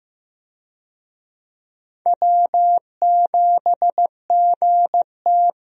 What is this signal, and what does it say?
Morse code, heard as a keyed tone, about 15 wpm, 700 Hz. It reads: W7GT
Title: USA Call Signs - 4 Characters - Repeated 3x - 15wpm